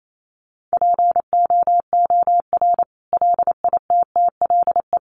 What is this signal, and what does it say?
Morse code, heard as a keyed tone, about 28 wpm, 700 Hz. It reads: POOR LITTLE
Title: Sentences from Top 1000 Words 28wpm